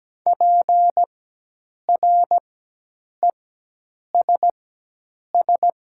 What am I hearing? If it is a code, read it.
Morse code, heard as a keyed tone, about 17 wpm, spaced as for 8 wpm, 700 Hz. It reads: PRESS
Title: 4x Character Spacing - Top 300 Words - 17wpm